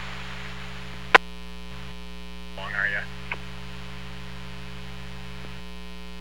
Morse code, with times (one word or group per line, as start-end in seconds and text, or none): none